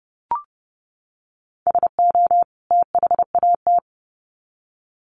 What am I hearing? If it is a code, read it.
Morse code, heard as a keyed tone, about 30 wpm, 700 Hz. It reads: SO THAT
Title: Sets of 2 Words 30wpm